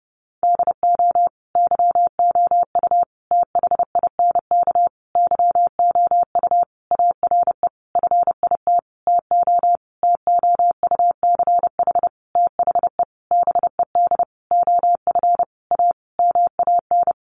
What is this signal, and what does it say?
Morse code, heard as a keyed tone, about 30 wpm, 700 Hz. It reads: DO YOU THINK YOU ARE FIT TO TOUCH THE BED OF A MAN
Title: Sentences from Top 1200 Words 30wpm